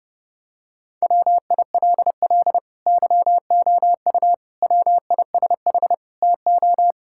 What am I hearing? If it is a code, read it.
Morse code, heard as a keyed tone, about 30 wpm, 700 Hz. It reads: WILL YOU WISH TO
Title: Sentences from Top 1600 Words 30wpm